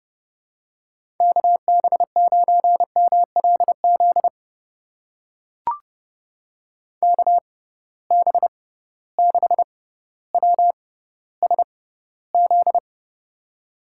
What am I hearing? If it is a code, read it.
Morse code, heard as a keyed tone, about 30 wpm, spaced as for 15 wpm, 700 Hz. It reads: KB9MLZ KB6WSZ
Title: USA Call Signs - 6 Characters - 6x Character Spacing - 30wpm